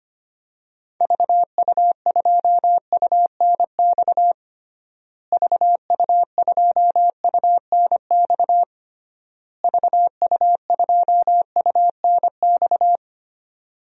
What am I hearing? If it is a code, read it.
Morse code, heard as a keyed tone, about 25 wpm, 700 Hz. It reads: VU2UNX VU2UNX VU2UNX
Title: DXCC-100 Call Signs - 6 Characters - Repeated 3x - 25wpm